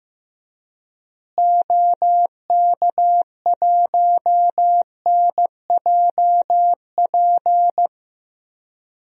Morse code, heard as a keyed tone, about 15 wpm, 700 Hz. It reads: OK1NJP
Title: DXCC-100 Call Signs - 6 Characters - Repeated 3x - 15wpm